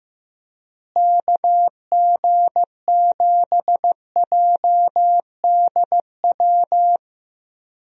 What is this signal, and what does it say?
Morse code, heard as a keyed tone, about 15 wpm, 700 Hz. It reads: KG7JDW
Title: USA Call Signs - 6 Characters - Repeated 2x - 15wpm